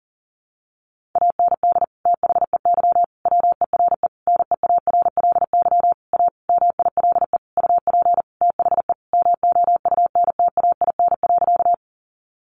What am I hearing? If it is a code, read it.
Morse code, heard as a keyed tone, about 40 wpm, 700 Hz. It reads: AND THEY WERE NEARLY A MILE UP THE MOUNTAIN.
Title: Sentences from Top 1300 Words 40wpm